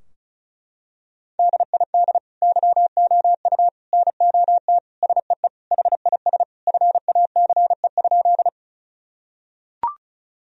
Morse code, heard as a keyed tone, about 36 wpm, 700 Hz. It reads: DID YOU NOT SEE HIS FACE?
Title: Sentences from Top 1000 Words - Review Part 2 - 35wpm